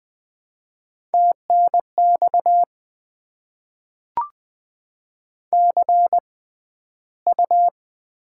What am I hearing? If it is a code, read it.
Morse code, heard as a keyed tone, about 20 wpm, spaced as for 12 wpm, 690 Hz. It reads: TNX CU
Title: Top 20 QSO Elements - 6x Character Spacing - 20wpm